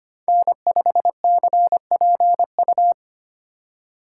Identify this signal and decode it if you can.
Morse code, heard as a keyed tone, about 25 wpm, 700 Hz. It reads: N5CPU
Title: USA Call Signs - 5 Characters - 3x Character Spacing - 25wpm